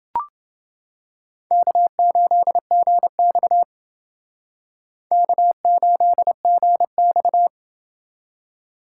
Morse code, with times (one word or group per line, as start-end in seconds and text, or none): 1.5-3.6 K8GX
5.1-7.5 K8GX